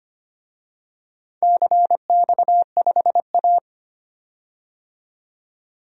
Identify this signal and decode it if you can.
Morse code, heard as a keyed tone, about 25 wpm, 700 Hz. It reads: CX5A
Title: DXCC-100 Call Signs - 4 Characters - 25wpm